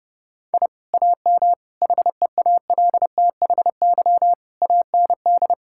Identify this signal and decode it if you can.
Morse code, heard as a keyed tone, about 32 wpm, 700 Hz. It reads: I AM HEALTHY AND
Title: Sentences from Top 1000 Words - Review Part 5 - 30wpm